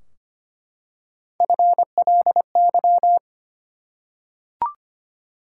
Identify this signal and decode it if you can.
Morse code, heard as a keyed tone, about 25 wpm, 700 Hz. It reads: FLY
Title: Three Letter Words - 25wpm